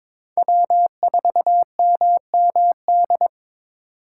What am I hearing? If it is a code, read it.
Morse code, heard as a keyed tone, about 22 wpm, 700 Hz. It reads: W4MMD
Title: K1USN SST QSO Practice - 22wpm